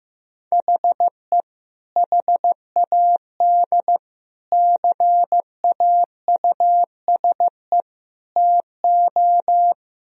Morse code, heard as a keyed tone, about 15 wpm, 700 Hz. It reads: HE HAD CAUSE TO
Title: Sentences from Top 1000 Words - Review Part 1 - 15wpm